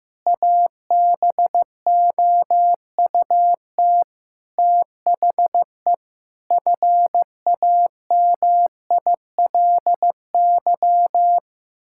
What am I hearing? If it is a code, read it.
Morse code, heard as a keyed tone, about 15 wpm, 700 Hz. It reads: ABOUT THE FAMILY
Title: Sentences from Top 1000 Words - Review Part 6 - 15wpm